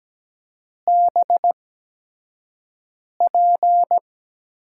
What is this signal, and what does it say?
Morse code, heard as a keyed tone, about 17 wpm, spaced as for 5 wpm, 700 Hz. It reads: BP
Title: USA Call Signs - 6 Characters - 8x Character Spacing - 17wpm